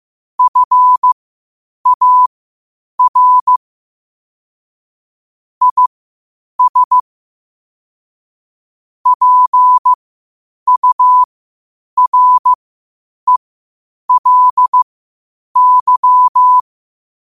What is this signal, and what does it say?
Morse code, heard as a keyed tone, about 15 wpm, spaced as for 8 wpm, 1000 Hz. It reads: FAR IS PURELY
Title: 8 WPM morse code quotes for Fri, 01 Mar 2024